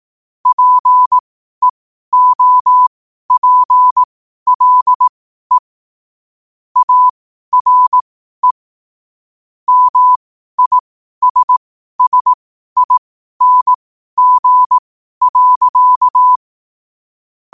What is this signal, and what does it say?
Morse code, heard as a keyed tone, about 18 wpm, spaced as for 12 wpm, 1000 Hz. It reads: PEOPLE ARE MISSING.
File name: qotdfiles / 12wpm